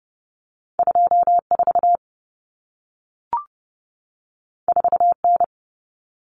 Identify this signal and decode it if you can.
Morse code, heard as a keyed tone, about 30 wpm, 700 Hz. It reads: 24 4N
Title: Common Cut Numbers - 2-Digit 30wpm